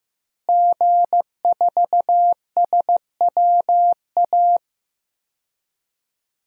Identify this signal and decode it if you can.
Morse code, heard as a keyed tone, about 15 wpm, 700 Hz. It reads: G4SWA
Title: DXCC-100 Call Signs - 5 Characters - Repeated 3x - 15wpm